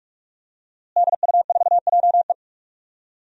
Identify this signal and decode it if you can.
Morse code, heard as a keyed tone, about 46 wpm, 700 Hz. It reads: DU4JE
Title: DXCC-100 Call Signs - 5 Characters - Repeated 3x - 45wpm